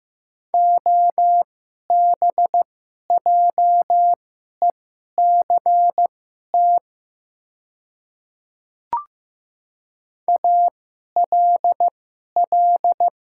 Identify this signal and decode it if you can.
Morse code, heard as a keyed tone, about 15 wpm, spaced as for 11 wpm, 700 Hz. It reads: OBJECT ALL
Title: 2x Character Spacing - Top 500 Words - 15wpm